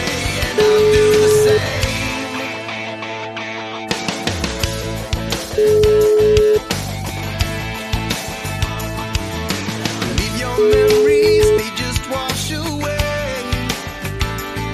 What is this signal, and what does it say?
Ringback tone (Europe, 425 Hz), with noise.